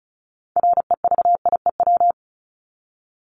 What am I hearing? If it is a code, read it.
Morse code, heard as a keyed tone, about 34 wpm, 700 Hz. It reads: REVIEW
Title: Top 751-1000 Words 35wpm